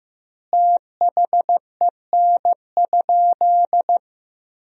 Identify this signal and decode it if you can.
Morse code, heard as a keyed tone, about 16 wpm, 700 Hz. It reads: THEN?
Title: 3x Word-Spacing - Sentences from Top 500 Words - 15wpm